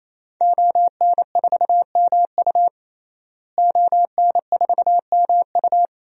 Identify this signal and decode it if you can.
Morse code, heard as a keyed tone, about 27 wpm, 700 Hz. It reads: ON4MU ON4MU
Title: DXCC-100 Call Signs - 5 Characters - Repeated 3x - 28wpm